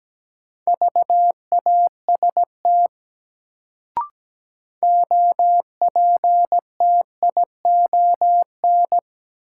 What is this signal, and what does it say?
Morse code, heard as a keyed tone, about 17 wpm, 700 Hz. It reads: VAST OPTION
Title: Sentences from Top 1300 Words - New Word Drill - 17wpm